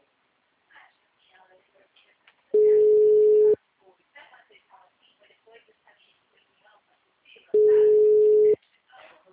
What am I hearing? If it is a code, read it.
Ringback tone (Europe, 425 Hz), with speech.